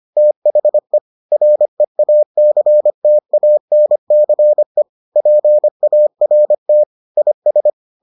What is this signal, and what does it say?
Morse code, heard as a keyed tone, about 25 wpm, 600 Hz. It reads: THE REACTANCE PART IS